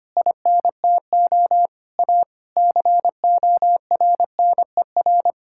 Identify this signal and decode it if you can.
Morse code, heard as a keyed tone, about 25 wpm, 700 Hz. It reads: INTO A CORNER
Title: Sentences from Top 1600 Words 25wpm